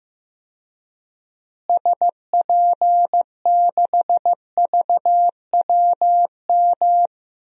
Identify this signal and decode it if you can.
Morse code, heard as a keyed tone, about 15 wpm, 700 Hz. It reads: SP6VWM
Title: DXCC-100 Call Signs - 6 Characters - Repeated 2x - 15wpm